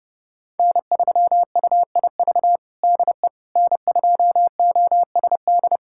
Morse code, heard as a keyed tone, about 32 wpm, 700 Hz. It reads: N3UIV DE N2OSD
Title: USA Callsign DE Callsign - 5 Characters - 30wpm